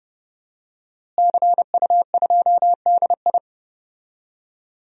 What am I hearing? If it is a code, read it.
Morse code, heard as a keyed tone, about 29 wpm, 700 Hz. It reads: CU2DI
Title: DXCC-100 Call Signs - 5 Characters - Repeated 2x - 30wpm